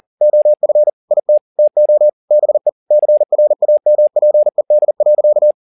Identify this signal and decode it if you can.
Morse code, heard as a keyed tone, about 40 wpm, 600 Hz. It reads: OF IT TO BE CRAMPED.